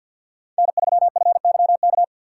Morse code, heard as a keyed tone, about 65 wpm, spaced as for 57 wpm, 700 Hz. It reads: N3FYK
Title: Speed Racing - USA 1x3 Call Signs 50wpm